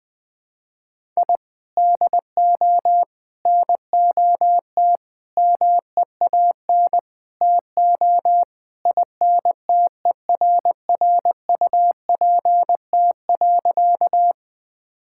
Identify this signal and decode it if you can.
Morse code, heard as a keyed tone, about 20 wpm, 700 Hz. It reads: I DO NOT MEAN TO INTERRUPT.
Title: Sentences with 5-Word Phrases - 20wpm